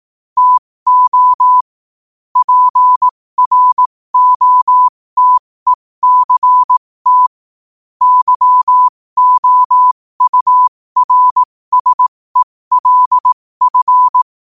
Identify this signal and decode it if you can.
Morse code, heard as a keyed tone, about 18 wpm, spaced as for 15 wpm, 1000 Hz. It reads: TO PROTECT YOURSELF
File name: qotdfiles / 15wpm